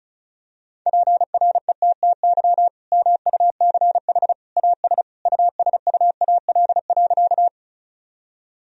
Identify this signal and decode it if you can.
Morse code, heard as a keyed tone, about 34 wpm, 700 Hz. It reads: PRETTY MUCH AS USUAL.
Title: Sentences from Top 1000 Words - Review Part 2 - 35wpm